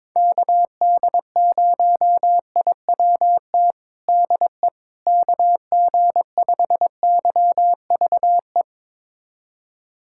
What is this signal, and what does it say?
Morse code, heard as a keyed tone, about 22 wpm, 700 Hz. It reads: KD0IWT DE KG5YVE
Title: USA Callsign DE Callsign - 6 Characters - 22wpm